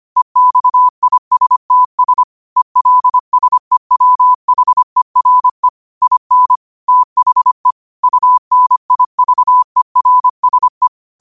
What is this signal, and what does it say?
Morse code, heard as a keyed tone, about 25 wpm, 1000 Hz. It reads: EXISTS ELSEWHERE IN THE UNIVERSE
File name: qotdfiles / 25wpm